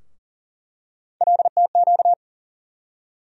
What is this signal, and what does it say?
Morse code, heard as a keyed tone, about 39 wpm, 700 Hz. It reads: LTQ
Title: USA Call Sign Postfixes - 3 Letters 40wpm